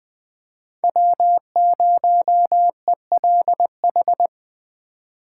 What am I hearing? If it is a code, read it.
Morse code, heard as a keyed tone, about 20 wpm, 700 Hz. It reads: W0ELH
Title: USA Call Signs - 5 Characters - 4x Character Spacing - 20wpm